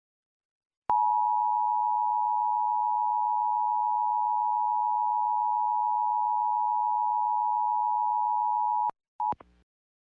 Emergency Alert System (SAME) tones (the 853 and 960 Hz attention tone). It is heard from 0.9 to 9.0 s.